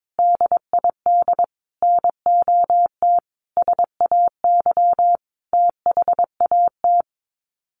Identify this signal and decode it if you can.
Morse code, heard as a keyed tone, about 22 wpm, 700 Hz. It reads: DID NOT SAY THAT
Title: Sets of 5 Words Encore2 22wpm